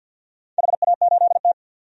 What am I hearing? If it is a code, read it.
Morse code, heard as a keyed tone, about 48 wpm, 700 Hz. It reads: HA8T
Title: DXCC-100 Call Signs - 4 Characters - 50wpm